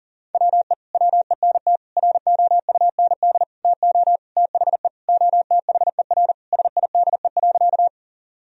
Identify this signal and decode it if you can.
Morse code, heard as a keyed tone, about 40 wpm, 700 Hz. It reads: WE WENT ROUND TO THE OTHER SIDE.
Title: Sentences from Top 1000 Words - Review Part 1 - 40wpm